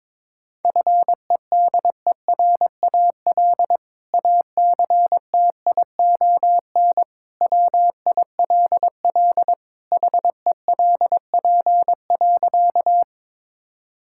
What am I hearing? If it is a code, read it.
Morse code, heard as a keyed tone, about 21 wpm, 700 Hz. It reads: FEDERAL ACTION WILL HELP.
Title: Sentences from Top 1000 Words - Review Part 5 - 22wpm